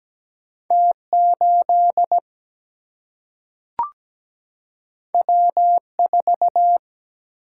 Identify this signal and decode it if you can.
Morse code, heard as a keyed tone, about 17 wpm, 700 Hz. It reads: T8 W4
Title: Single-Letter Single-Number 17wpm